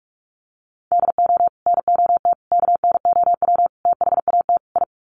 Morse code, heard as a keyed tone, about 44 wpm, 700 Hz. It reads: DO NOT KNOW THAT I
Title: Sets of 5 Words Encore2 45wpm